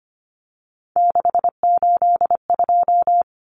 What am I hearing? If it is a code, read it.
Morse code, heard as a keyed tone, about 25 wpm, 700 Hz. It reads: 682
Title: Common Cut Numbers - 3-Digit 25wpm